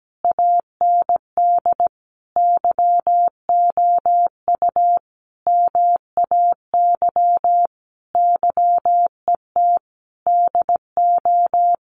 Morse code, heard as a keyed tone, about 17 wpm, 700 Hz. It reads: AND YOU MAY YET DO
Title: Sentences from Top 1400 Words 17wpm